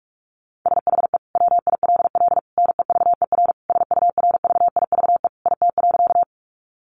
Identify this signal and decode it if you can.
Morse code, heard as a keyed tone, about 44 wpm, 700 Hz. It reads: SHE WILL NEVER SURVIVE IT.
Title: Sentences from Top 1300 Words 45wpm